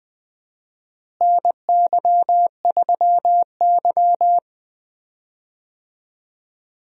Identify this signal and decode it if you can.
Morse code, heard as a keyed tone, about 20 wpm, 700 Hz. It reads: NY3Y
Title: USA Call Signs - 4 Characters - Repeated 3x - 20wpm